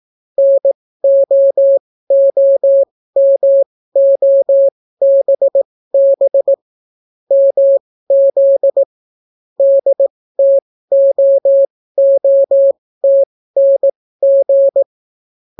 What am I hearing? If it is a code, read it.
Morse code, heard as a keyed tone, about 18 wpm, spaced as for 14 wpm, 550 Hz. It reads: NOOMOBB MZ DTOOTNG